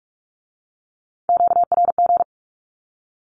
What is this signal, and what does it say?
Morse code, heard as a keyed tone, about 46 wpm, 700 Hz. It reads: QRG
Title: Q-Codes with Spoken Meaning - 45wpm